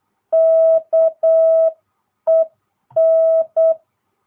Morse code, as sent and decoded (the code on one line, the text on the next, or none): -.- . -.
KEN